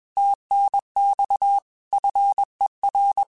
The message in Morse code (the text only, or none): TNX FER